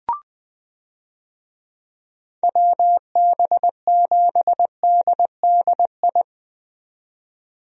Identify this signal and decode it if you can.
Morse code, heard as a keyed tone, about 20 wpm, 700 Hz. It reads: WB7DDI
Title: USA Call Signs - 6 Characters - 20wpm